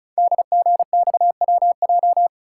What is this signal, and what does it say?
Morse code, heard as a keyed tone, about 36 wpm, 690 Hz. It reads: DGXWJ
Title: Code Groups - 5 Letters - 35wpm